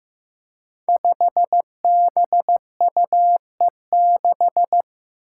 Morse code, heard as a keyed tone, about 15 wpm, 700 Hz. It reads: 5BUE6
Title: Five Challenging Letter-Number Combinations 15wpm